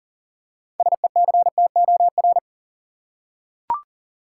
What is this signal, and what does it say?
Morse code, heard as a keyed tone, about 37 wpm, 700 Hz. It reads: SECTOR
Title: Six Letter Words - 40wpm